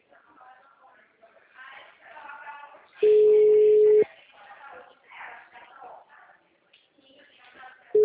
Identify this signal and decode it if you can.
Ringback tone (Europe, 425 Hz), with speech.
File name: out- -2010- - - .14369